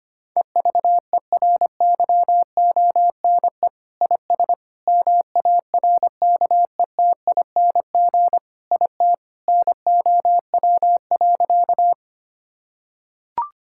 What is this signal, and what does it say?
Morse code, heard as a keyed tone, about 25 wpm, 700 Hz. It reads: EVERYONE IS MARKETING IT NOW.
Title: Sentences from Top 1000 Words - Review Part 5 - 25wpm